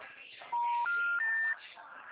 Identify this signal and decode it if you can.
Special information tone, with speech.